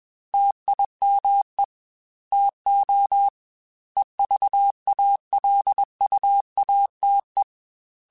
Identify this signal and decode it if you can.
Morse code, heard as a keyed tone, about 21 wpm, 800 Hz. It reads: TIME TO EVALUATE
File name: Technical Subjects / Technical Subjects-mp3s / AND Gate @20